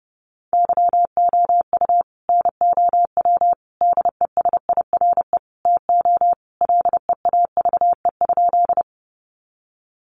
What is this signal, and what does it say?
Morse code, heard as a keyed tone, about 30 wpm, 700 Hz. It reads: YOU NOW DESIRE TO LEAVE?
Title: Sentences from Top 1100 Words 30wpm